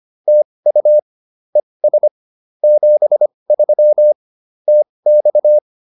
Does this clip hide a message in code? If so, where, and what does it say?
Yes: Morse code from 0.3 to 5.6 s, heard as a keyed tone, about 25 wpm, spaced as for 20 wpm, 600 Hz. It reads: TU ES 73 TX